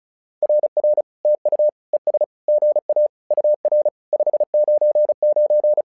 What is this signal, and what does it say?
Morse code, heard as a keyed tone, about 36 wpm, 600 Hz. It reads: RR TU ES GA UR 599
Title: SOTA QSO Practice 35wpm